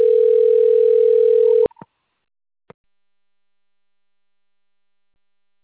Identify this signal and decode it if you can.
Ringback tone (North America), 440 + 480 Hz.